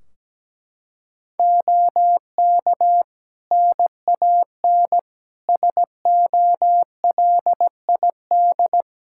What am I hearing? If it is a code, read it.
Morse code, heard as a keyed tone, about 17 wpm, 700 Hz. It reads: OK NAN SOLID
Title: Practice QSO - Exchange 3 - More Info - 17wpm